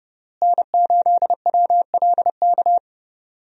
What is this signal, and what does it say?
Morse code, heard as a keyed tone, about 30 wpm, 700 Hz. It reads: N8WLK
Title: USA Call Signs - 5 Characters - Repeated 3x - 30wpm